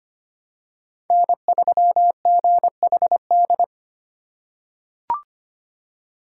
Morse code, heard as a keyed tone, about 25 wpm, 700 Hz. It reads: N3GHD
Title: USA Call Signs - 2x Character Spacing - 25wpm